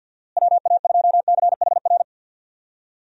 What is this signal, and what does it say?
Morse code, heard as a keyed tone, about 51 wpm, 700 Hz. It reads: WA2CHR
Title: USA Call Signs - 6 Characters - Repeated 2x - 50wpm